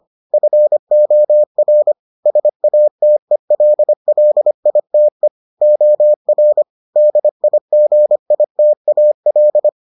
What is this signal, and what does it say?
Morse code, heard as a keyed tone, about 25 wpm, 600 Hz. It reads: FOR SATELLITE OR DIGITAL